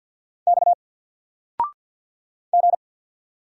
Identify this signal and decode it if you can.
Morse code, heard as a keyed tone, about 50 wpm, 700 Hz. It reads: X G
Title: Single Letter-Number 50wpm